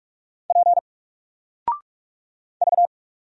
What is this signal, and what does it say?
Morse code, heard as a keyed tone, about 44 wpm, 700 Hz. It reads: P V
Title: Single Letter-Number 45wpm